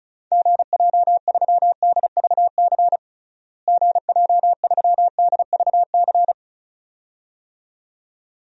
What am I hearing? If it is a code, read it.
Morse code, heard as a keyed tone, about 34 wpm, 700 Hz. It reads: GJ3DVC GJ3DVC
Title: DXCC-100 Call Signs - 6 Characters - Repeated 2x - 35wpm